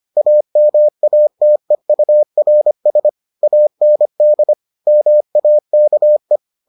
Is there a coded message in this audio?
Yes: Morse code from 0.2 to 6.4 s, heard as a keyed tone, about 25 wpm, 600 Hz. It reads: AMATEURS AND MAKE